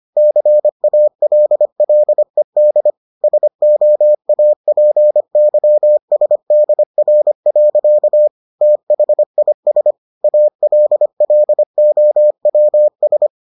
Morse code, heard as a keyed tone, about 25 wpm, 600 Hz. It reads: CALLED SOAPYSDR. THIS ALLOWS